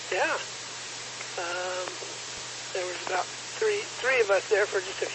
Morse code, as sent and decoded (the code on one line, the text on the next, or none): none